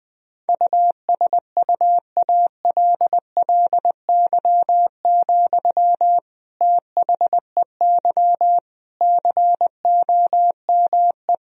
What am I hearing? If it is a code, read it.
Morse code, heard as a keyed tone, about 20 wpm, 700 Hz. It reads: USUALLY, THEY COME
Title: Sentences from Top 1000 Words - Review Part 3 - 20wpm